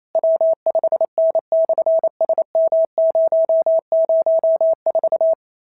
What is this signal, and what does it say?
Morse code, heard as a keyed tone, about 28 wpm, 650 Hz. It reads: W5N/SM004
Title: SOTA QSO Practice 28wpm